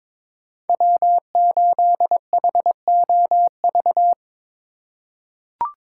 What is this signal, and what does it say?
Morse code, heard as a keyed tone, about 22 wpm, 700 Hz. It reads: W8HOV
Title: USA Call Signs - 5 Characters - 3x Character Spacing - 22wpm